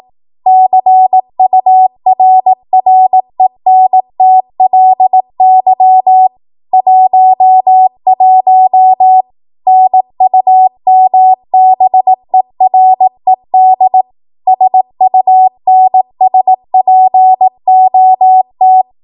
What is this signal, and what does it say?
Morse code, heard as a keyed tone, about 18 wpm, 750 Hz. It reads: CURRENTLY 11 NUMBERED SUNSPOT
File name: Morse / 18WPM